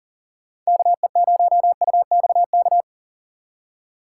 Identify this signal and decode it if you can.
Morse code, heard as a keyed tone, about 40 wpm, 700 Hz. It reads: KE0UXK
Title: USA Call Signs - 6 Characters - 2x Character Spacing - 40wpm